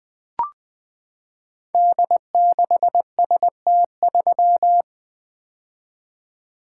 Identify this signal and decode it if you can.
Morse code, heard as a keyed tone, about 20 wpm, 700 Hz. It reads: D6ST3
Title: Code Groups - 5 Letter:Numbers - 20wpm